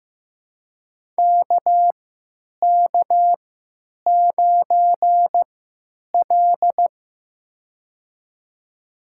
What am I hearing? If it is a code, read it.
Morse code, heard as a keyed tone, about 15 wpm, spaced as for 9 wpm, 700 Hz. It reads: KK9L
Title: USA Call Signs - 4 Characters - 3x Character Spacing - 15wpm